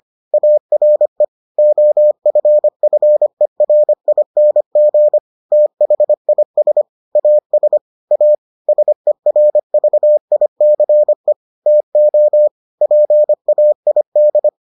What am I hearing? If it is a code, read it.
Morse code, heard as a keyed tone, about 25 wpm, 600 Hz. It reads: ARE OFFERING THIS AS A SERVICE TO PAID